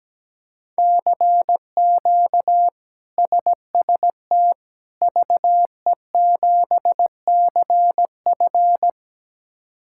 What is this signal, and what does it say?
Morse code, heard as a keyed tone, about 17 wpm, 700 Hz. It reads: CQ SST VE7CF
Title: K1USN SST QSO Practice - 17wpm